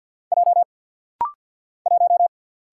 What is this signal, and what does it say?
Morse code, heard as a keyed tone, about 50 wpm, 700 Hz. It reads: J 1
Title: Single Letter-Number - Rapid-Fire 50wpm